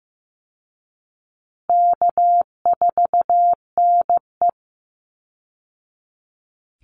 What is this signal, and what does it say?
Morse code, heard as a keyed tone, about 15 wpm, 700 Hz. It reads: K4NE
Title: Easy Call Signs Repeated 2x - 15wpm